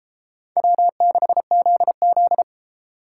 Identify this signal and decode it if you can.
Morse code, heard as a keyed tone, about 31 wpm, 700 Hz. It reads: W6ZZ